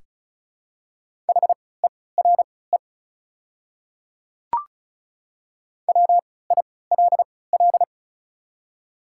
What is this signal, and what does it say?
Morse code, heard as a keyed tone, about 36 wpm, spaced as for 20 wpm, 690 Hz. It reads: HERE WILL
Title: 3x Character Spacing - Top 500 Words - 35wpm